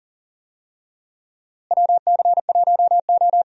Morse code, heard as a keyed tone, about 40 wpm, 700 Hz. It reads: WC1O
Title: USA Call Signs - 4 Characters - Repeated 2x - 40wpm